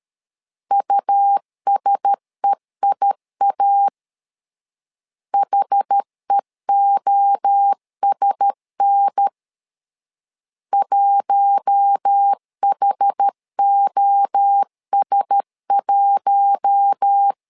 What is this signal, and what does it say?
Morse code, heard as a keyed tone, about 13 wpm, 800 Hz. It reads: USEIA HEOSN 1HOS1
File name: LEZIONI / 4 Q1S 12wpm 1802